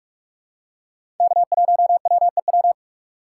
Morse code, heard as a keyed tone, about 47 wpm, 700 Hz. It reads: K1WEW
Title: USA Call Signs - 4x Character Spacing - 45wpm